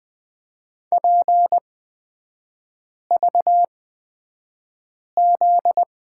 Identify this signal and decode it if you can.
Morse code, heard as a keyed tone, about 20 wpm, spaced as for 5 wpm, 700 Hz. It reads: PVZ
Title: 243 Lesson 39e - Letter z - Three Characters 20wpm